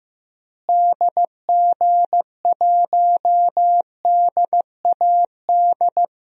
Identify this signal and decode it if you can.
Morse code, heard as a keyed tone, about 16 wpm, 700 Hz. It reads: DG1DAD
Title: DXCC-100 Call Signs - 6 Characters - Repeated 2x - 15wpm